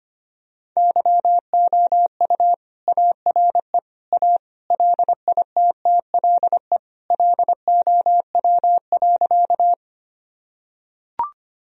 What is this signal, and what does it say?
Morse code, heard as a keyed tone, about 25 wpm, 700 Hz. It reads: YOU ARE A LITTLE LOW.
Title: Sentences from Top 1000 Words - Review Part 1 - 25wpm